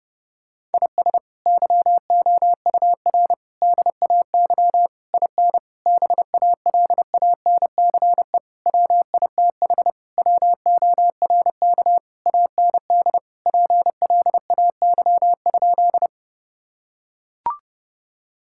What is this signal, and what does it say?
Morse code, heard as a keyed tone, about 32 wpm, 700 Hz. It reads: IS YOUR DAY IN BALANCE WITH WORK AND PLAY?
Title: Sentences from Top 1000 Words - Review Part 6 - 30wpm